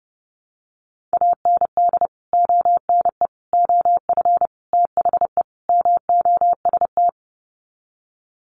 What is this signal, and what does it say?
Morse code, heard as a keyed tone, about 30 wpm, 700 Hz. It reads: AND ONE OF THE MOST
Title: Sets of 5 Words Encore 30wpm